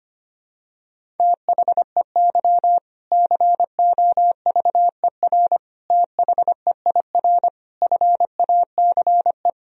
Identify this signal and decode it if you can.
Morse code, heard as a keyed tone, about 25 wpm, 700 Hz. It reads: THEY COVER THEIR FACE
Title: Sentences from Top 1000 Words - Review Part 1 - 25wpm